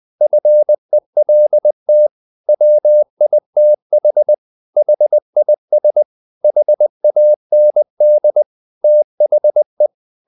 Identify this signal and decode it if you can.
Morse code, heard as a keyed tone, about 20 wpm, 600 Hz. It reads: FELT WITH HIS HAND THE
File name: the old man and the sea / the old man and the sea part 1 20wpm